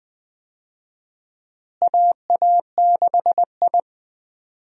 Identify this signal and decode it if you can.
Morse code, heard as a keyed tone, about 20 wpm, 700 Hz. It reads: AA6I
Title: USA Call Signs - 4 Characters - Repeated 3x - 20wpm